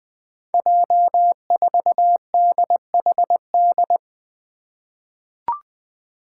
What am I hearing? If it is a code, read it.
Morse code, heard as a keyed tone, about 21 wpm, 700 Hz. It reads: J4DHD
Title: Five Challenging Letter-Number Combinations 20wpm